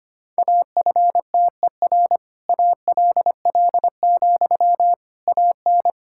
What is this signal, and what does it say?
Morse code, heard as a keyed tone, about 25 wpm, 700 Hz. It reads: AFTER ALL, AN
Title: Sentences from Top 1000 Words - Review Part 6 - 25wpm